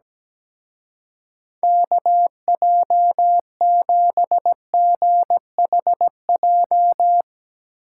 Morse code, heard as a keyed tone, about 17 wpm, 700 Hz. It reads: KJ7GHJ
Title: Speed Racing - USA 2x3 Call Signs 17wpm